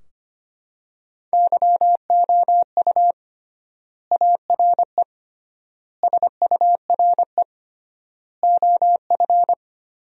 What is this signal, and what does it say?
Morse code, heard as a keyed tone, about 25 wpm, 700 Hz. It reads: YOU ARE SURE OF
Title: 3x Word-Spacing - Sentences from Top 300 Words - 25wpm